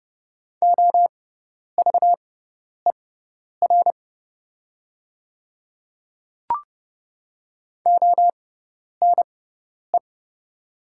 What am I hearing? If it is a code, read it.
Morse code, heard as a keyed tone, about 30 wpm, spaced as for 10 wpm, 700 Hz. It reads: OVER ONE
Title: 6x Character Spacing - Top 200 Words - 30wpm